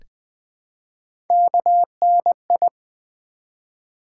Morse code, heard as a keyed tone, about 20 wpm, 700 Hz. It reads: KNI
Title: Build A Word - 1 Letter At A Time - 5-Letter Words - 20wpm